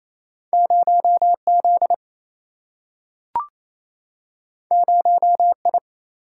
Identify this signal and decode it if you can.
Morse code, heard as a keyed tone, about 28 wpm, 700 Hz. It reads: 0 Z 0 I
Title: Single-Letter Single-Number 28wpm